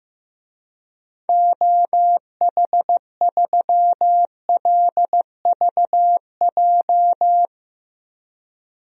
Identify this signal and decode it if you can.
Morse code, heard as a keyed tone, about 15 wpm, 700 Hz. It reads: OH3LVJ
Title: DXCC-100 Call Signs - 6 Characters - 15wpm